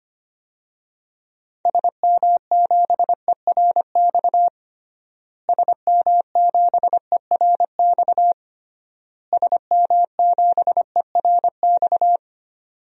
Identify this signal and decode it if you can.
Morse code, heard as a keyed tone, about 25 wpm, 700 Hz. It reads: SM7ERX SM7ERX SM7ERX
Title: DXCC-100 Call Signs - 6 Characters - Repeated 3x - 25wpm